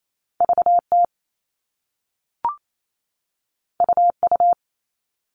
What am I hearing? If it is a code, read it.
Morse code, heard as a keyed tone, about 28 wpm, 700 Hz. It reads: VT UU
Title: All Cut Numbers - 2-Digit 28wpm